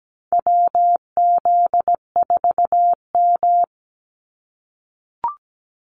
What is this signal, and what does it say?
Morse code, heard as a keyed tone, about 17 wpm, 700 Hz. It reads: WZ4M
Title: NA SOTA Activator Callsigns 17wpm